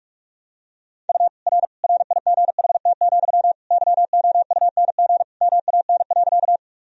Morse code, heard as a keyed tone, about 41 wpm, 700 Hz. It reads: U R RIGHT, YOUNG MAN.
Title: Sentences from Top 400 Words and Abbreviations 45wpm